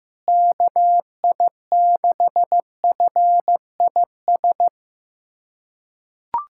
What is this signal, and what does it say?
Morse code, heard as a keyed tone, about 15 wpm, 700 Hz. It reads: KI6FIS
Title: USA Callsign DE Callsign 15wpm